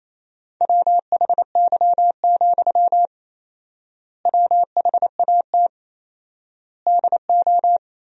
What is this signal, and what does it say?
Morse code, heard as a keyed tone, about 27 wpm, 700 Hz. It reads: WHY, WHAT DO
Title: 4x Word-Spacing - Sentences from Top 200 Words - 28wpm